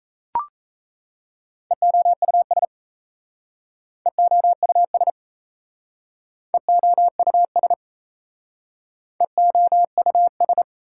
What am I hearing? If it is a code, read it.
Morse code, heard as a keyed tone, about 38 wpm, 700 Hz. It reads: EOUS EOUS EOUS EOUS